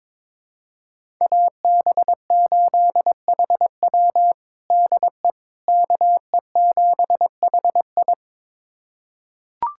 Morse code, heard as a keyed tone, about 22 wpm, 700 Hz. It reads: AB8HW DE KE7HI
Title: USA Callsign DE Callsign - 5 Characters - 22wpm